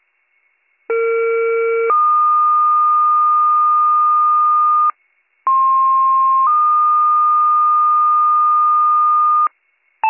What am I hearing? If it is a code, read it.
Two-tone page, heard as a one-second tone, then a three-second tone, about 460 then 1220 Hz, about 1060 then 1220 Hz. It is heard from 0.9 to 9.5 s.